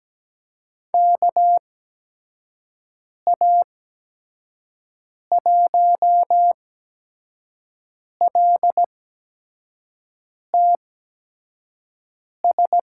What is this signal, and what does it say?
Morse code, heard as a keyed tone, about 17 wpm, spaced as for 5 wpm, 700 Hz. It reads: KA1LTS
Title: USA Call Signs - 6 Characters - 8x Character Spacing - 17wpm